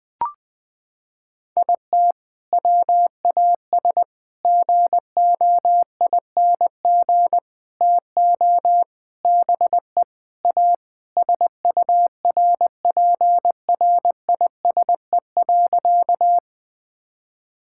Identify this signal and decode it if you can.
Morse code, heard as a keyed tone, about 20 wpm, 700 Hz. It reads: IT WAS GOING TO BE A SURPRISE.
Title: Sentences with 4-Word Phrases - 20wpm